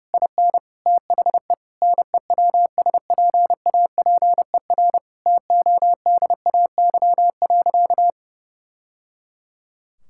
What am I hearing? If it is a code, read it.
Morse code, heard as a keyed tone, about 30 wpm, 700 Hz. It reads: IN THE NEWSPAPER TODAY.
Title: Sentences from Top 1000 Words - Review Part 6 - 30wpm